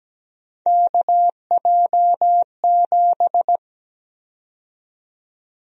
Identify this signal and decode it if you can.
Morse code, heard as a keyed tone, about 17 wpm, 700 Hz. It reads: KJ7
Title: USA Call Sign Prefixes - 17wpm